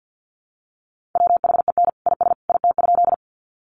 Morse code, heard as a keyed tone, about 50 wpm, 700 Hz. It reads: WHEN IS IT?
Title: Sentences from Top 100 Words 50wpm